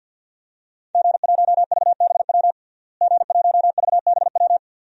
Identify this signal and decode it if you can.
Morse code, heard as a keyed tone, about 52 wpm, 700 Hz. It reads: G1VBW G1VBW
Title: DXCC-100 Call Signs - 5 Characters - Repeated 2x - 50wpm